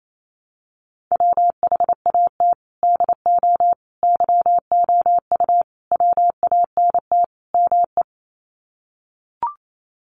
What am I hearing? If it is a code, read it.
Morse code, heard as a keyed tone, about 28 wpm, 700 Hz. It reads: WHAT DO YOU WANT ME
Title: Sets of 5 Words Encore2 28wpm